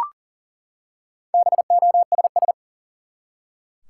Morse code, heard as a keyed tone, about 40 wpm, 700 Hz. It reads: BOSS
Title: Four Letter Words - 40wpm